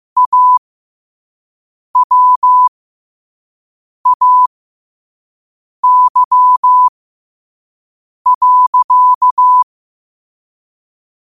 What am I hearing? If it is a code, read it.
Morse code, heard as a keyed tone, about 15 wpm, spaced as for 5 wpm, 1000 Hz. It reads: AWAY.
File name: qotdfiles / 5wpm